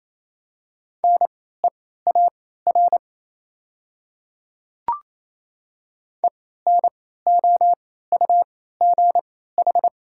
Morse code, heard as a keyed tone, about 28 wpm, spaced as for 16 wpm, 700 Hz. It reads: NEAR ENOUGH
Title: 3x Character Spacing - Top 500 Words - 28wpm